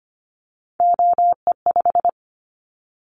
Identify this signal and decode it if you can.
Morse code, heard as a keyed tone, about 25 wpm, 700 Hz. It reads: OE5
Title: Callsign Prefixes for Top 50 DXCC - 25wpm